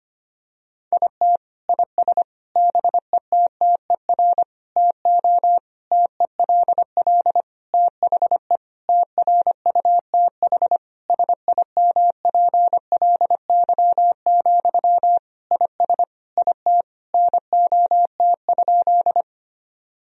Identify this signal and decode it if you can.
Morse code, heard as a keyed tone, about 25 wpm, 700 Hz. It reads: IT IS BETTER TO TELL THE TRUTH SIMPLY, IS IT NOT?
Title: Sentences from Top 1000 Words - Review Part 4 - 25wpm